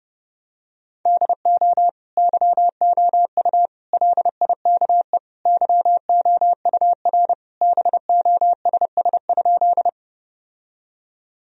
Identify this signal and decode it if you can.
Morse code, heard as a keyed tone, about 31 wpm, 700 Hz. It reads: DO YOU LIKE YOUR BOSS?
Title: Sentences from Top 1000 Words - Review Part 4 - 30wpm